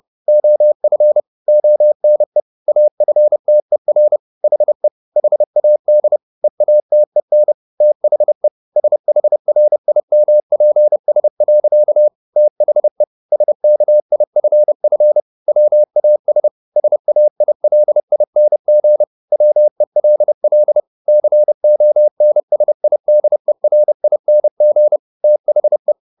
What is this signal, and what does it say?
Morse code, heard as a keyed tone, about 30 wpm, 600 Hz. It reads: OF ONE AFTER HE HAD EATEN THE SHRIMPS. THE SKIFF WAS SAILING WELL CONSIDERING THE